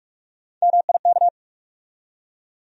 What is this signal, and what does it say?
Morse code, heard as a keyed tone, about 46 wpm, 700 Hz. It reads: MIK
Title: USA Call Sign Postfixes - 3 Letters 45wpm